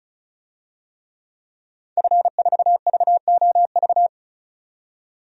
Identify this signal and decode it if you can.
Morse code, heard as a keyed tone, about 34 wpm, 700 Hz. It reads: F4VOV
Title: DXCC-100 Call Signs - 5 Characters - 35wpm